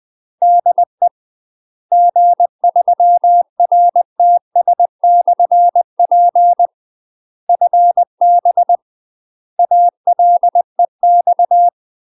Morse code, heard as a keyed tone, about 20 wpm, 700 Hz. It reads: DE G3RTS/P FB ALEX
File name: UK MASTER QSOs TRAINING / 20WPM / UK MASTER QSOs 0042